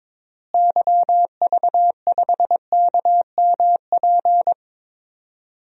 Morse code, heard as a keyed tone, about 22 wpm, 700 Hz. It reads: YV5KMP
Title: DXCC-100 Call Signs - 6 Characters - Repeated 2x - 22wpm